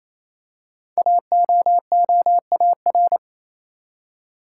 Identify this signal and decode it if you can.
Morse code, heard as a keyed tone, about 27 wpm, 700 Hz. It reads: AOOAR
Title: Code Groups - 5 Letter:Numbers - 28wpm